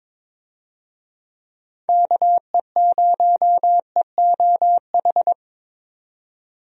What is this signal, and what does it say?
Morse code, heard as a keyed tone, about 22 wpm, 700 Hz. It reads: KE0EOH
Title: USA Call Signs - 6 Characters - 22wpm